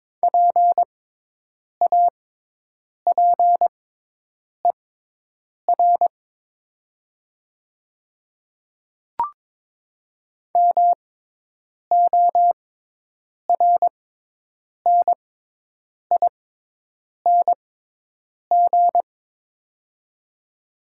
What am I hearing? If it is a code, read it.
Morse code, heard as a keyed tone, about 22 wpm, spaced as for 8 wpm, 700 Hz. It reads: PAPER MORNING
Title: 6x Character Spacing - Top 500 Words - 22wpm